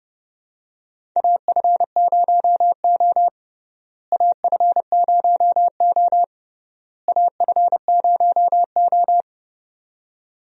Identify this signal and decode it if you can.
Morse code, heard as a keyed tone, about 30 wpm, 700 Hz. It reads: AF0O AF0O AF0O
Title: USA Call Signs - 4 Characters - Repeated 3x - 30wpm